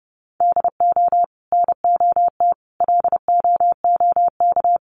Morse code, heard as a keyed tone, about 30 wpm, 700 Hz. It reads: DO NOT LOOK
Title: Sentences from Top 1100 Words 30wpm